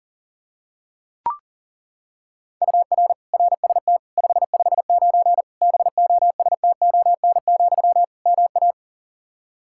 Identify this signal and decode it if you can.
Morse code, heard as a keyed tone, about 40 wpm, 700 Hz. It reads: UR RST 559 BOSTON, MA
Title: Practice QSO - Exchange 1 - Beginning - 40wpm